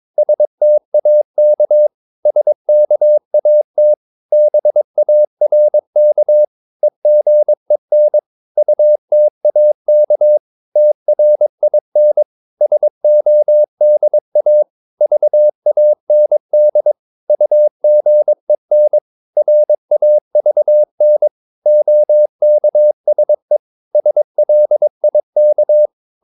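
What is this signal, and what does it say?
Morse code, heard as a keyed tone, about 22 wpm, 600 Hz. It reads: STAK SKAT BARK EGEN UTAK TRIN SODA VAND UGEN RAVN OKSE SLIK
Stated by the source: Morse practice of short Danish words at 22 wpm, from CW med Gnister.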